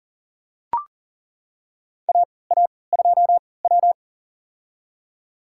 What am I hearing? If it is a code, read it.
Morse code, heard as a keyed tone, about 40 wpm, spaced as for 23 wpm, 700 Hz. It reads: AA2W
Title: USA Call Signs - 4 Characters - 3x Character Spacing - 40wpm